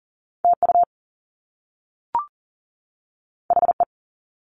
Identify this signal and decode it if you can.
Morse code, heard as a keyed tone, about 39 wpm, 700 Hz. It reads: TU HE
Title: Top Two Letter Combinations in Words 40wpm